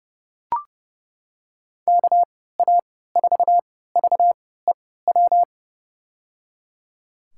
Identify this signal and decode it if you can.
Morse code, heard as a keyed tone, about 30 wpm, spaced as for 17 wpm, 700 Hz. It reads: KA4VEW
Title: USA Call Signs - 3x Character Spacing - 30wpm